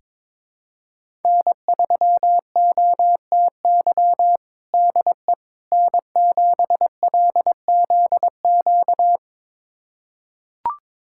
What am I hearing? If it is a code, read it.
Morse code, heard as a keyed tone, about 22 wpm, 700 Hz. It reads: N3OTY DE N7LZQ
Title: USA Callsign DE Callsign - 5 Characters - 22wpm